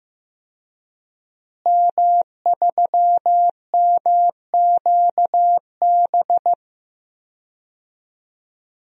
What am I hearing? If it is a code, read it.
Morse code, heard as a keyed tone, about 15 wpm, 700 Hz. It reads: M3MQB
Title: DXCC-100 Call Signs - 5 Characters - Repeated 3x - 15wpm